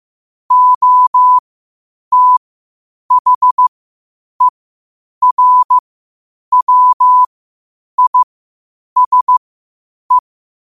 Morse code, heard as a keyed tone, about 15 wpm, spaced as for 8 wpm, 1000 Hz. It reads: OTHERWISE
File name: qotdfiles / 8wpm